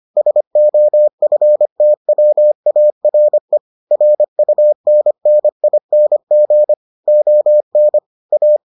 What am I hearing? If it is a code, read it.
Morse code, heard as a keyed tone, about 25 wpm, 600 Hz. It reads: SOFTWARE RUNNING ON A